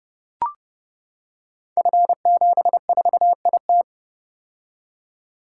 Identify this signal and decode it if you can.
Morse code, heard as a keyed tone, about 28 wpm, 700 Hz. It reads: F74IT
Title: Code Groups - 5 Letter:Numbers - 30wpm